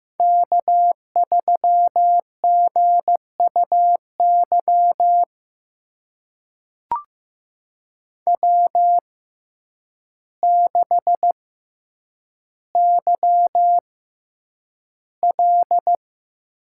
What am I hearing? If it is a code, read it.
Morse code, heard as a keyed tone, about 15 wpm, spaced as for 8 wpm, 700 Hz. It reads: K3GUY W6YL
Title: USA Call Signs - 5 Characters - 6x Character Spacing - 15wpm